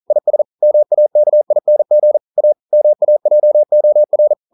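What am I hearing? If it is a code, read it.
Morse code, heard as a keyed tone, about 38 wpm, 600 Hz. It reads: IS MAKING A MAJOR
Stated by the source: Morse code practice at 41 wpm, from CW med Gnister.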